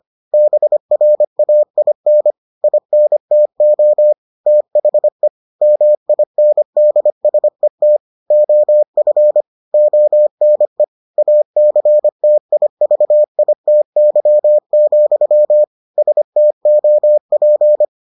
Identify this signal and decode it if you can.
Morse code, heard as a keyed tone, about 25 wpm, 600 Hz. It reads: BRAIN INTO THE MINDSET OF ONE ACTIVITY, STOP